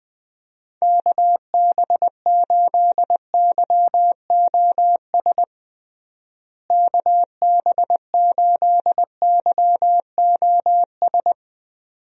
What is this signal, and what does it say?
Morse code, heard as a keyed tone, about 20 wpm, 700 Hz. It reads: KB8YOS KB8YOS
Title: USA Call Signs - 6 Characters - Repeated 2x - 20wpm